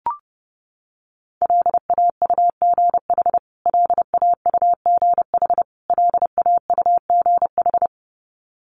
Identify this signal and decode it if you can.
Morse code, heard as a keyed tone, about 28 wpm, 700 Hz. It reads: LAUGH LAUGH LAUGH
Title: Top 500 Words Repeated 30wpm